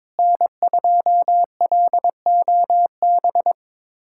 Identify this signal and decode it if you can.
Morse code, heard as a keyed tone, about 22 wpm, 700 Hz. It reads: N2LOB
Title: USA Callsign DE Callsign - 5 Characters - 22wpm